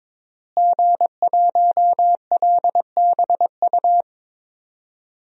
Morse code, heard as a keyed tone, about 22 wpm, 700 Hz. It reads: G1LBU
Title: DXCC-100 Call Signs - 5 Characters - Repeated 2x - 22wpm